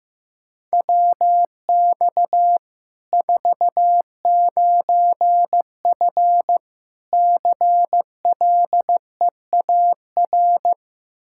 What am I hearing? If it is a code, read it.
Morse code, heard as a keyed tone, about 15 wpm, 700 Hz. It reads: WX 49F CLEAR
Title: Practice QSO - Exchange 2 - Info - 15wpm